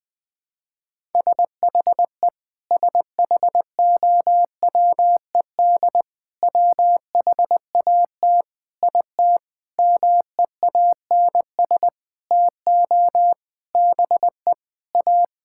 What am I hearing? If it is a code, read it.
Morse code, heard as a keyed tone, about 20 wpm, 700 Hz. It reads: SHE SHOWED WHAT IT MEANS TO BE A
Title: Sentences with 6-Word Phrases - 20wpm